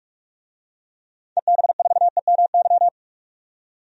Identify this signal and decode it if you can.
Morse code, heard as a keyed tone, about 46 wpm, 700 Hz. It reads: EB4EMY
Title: DXCC-100 Call Signs - 6 Characters - 45wpm